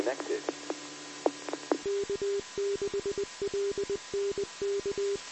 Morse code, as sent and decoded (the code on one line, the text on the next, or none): none